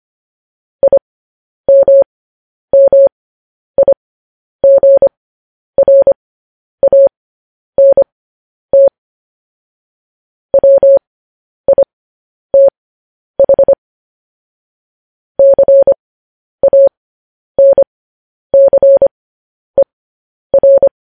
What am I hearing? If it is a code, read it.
Morse code, heard as a keyed tone, about 25 wpm, spaced as for 10 wpm, 550 Hz. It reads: IMMIGRANT WITH CANCER